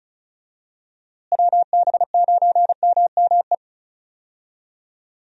Morse code, heard as a keyed tone, about 36 wpm, 700 Hz. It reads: WB9MME
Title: USA Call Signs - 6 Characters - Repeated 3x - 35wpm